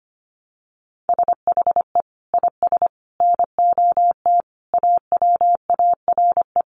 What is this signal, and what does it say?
Morse code, heard as a keyed tone, about 25 wpm, 700 Hz. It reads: SHE IS NOT AWARE
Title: Sentences from Top 1200 Words 25wpm